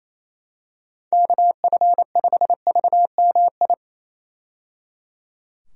Morse code, heard as a keyed tone, about 28 wpm, 700 Hz. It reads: KF5VMI
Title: USA Call Signs - 6 Characters - 28wpm